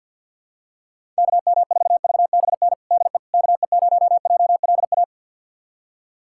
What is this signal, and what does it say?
Morse code, heard as a keyed tone, about 50 wpm, 700 Hz. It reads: KM4VBN DE KE0JLA
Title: USA Callsign DE Callsign - 6 Characters - 50wpm